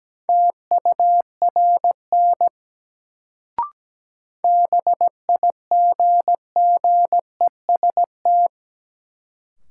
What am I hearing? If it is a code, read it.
Morse code, heard as a keyed tone, about 17 wpm, 700 Hz. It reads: TURN BIGGEST
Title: Sets of 4 Words - New Word Drill - 17wpm